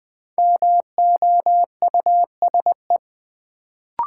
Morse code, heard as a keyed tone, about 20 wpm, 700 Hz. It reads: MOUSE
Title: Sentences from Top 700 Words - New Word Drill - 20wpm